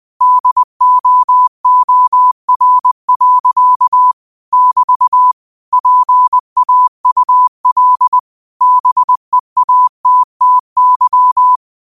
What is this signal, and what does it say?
Morse code, heard as a keyed tone, about 20 wpm, 1000 Hz. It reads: DOOR. <BT> PAUL BEATTY
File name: qotdfiles / 20wpm